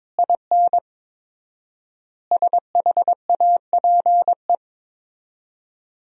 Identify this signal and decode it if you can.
Morse code, heard as a keyed tone, about 22 wpm, 700 Hz. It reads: IN SHAPE
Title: 4x Word-Spacing - Sentences from Top 500 Words - 22wpm